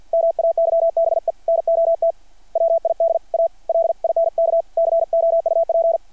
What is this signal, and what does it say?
Morse code, heard as a keyed tone, about 52 wpm, 660 Hz. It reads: MAYBE NOT WID A LINK COUP
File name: archived / CRSnet